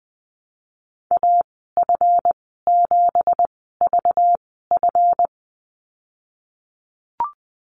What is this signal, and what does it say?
Morse code, heard as a keyed tone, about 20 wpm, spaced as for 15 wpm, 700 Hz. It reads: AF7VF
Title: 193 Lesson 31g - Letter v - Call Signs 20wpm